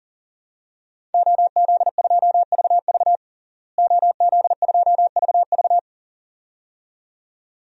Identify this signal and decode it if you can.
Morse code, heard as a keyed tone, about 40 wpm, 700 Hz. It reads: OZ2VV OZ2VV
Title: DXCC-100 Call Signs - 5 Characters - Repeated 2x - 40wpm